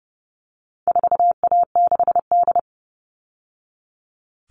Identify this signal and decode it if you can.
Morse code, heard as a keyed tone, about 30 wpm, 700 Hz. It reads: 4A6D
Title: All Cut Numbers - 4-Digit 30wpm